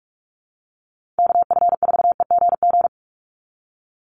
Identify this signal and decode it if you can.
Morse code, heard as a keyed tone, about 44 wpm, 700 Hz. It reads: KF4EGG
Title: USA Call Signs 45wpm